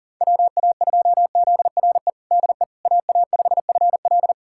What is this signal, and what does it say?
Morse code, heard as a keyed tone, about 37 wpm, 700 Hz. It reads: WA2ZRE DE AA5FL
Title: USA Callsign DE Callsign 40wpm